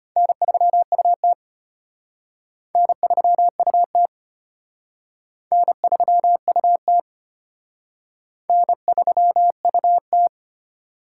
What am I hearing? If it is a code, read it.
Morse code, heard as a keyed tone, about 32 wpm, 700 Hz. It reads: N3UT N3UT N3UT N3UT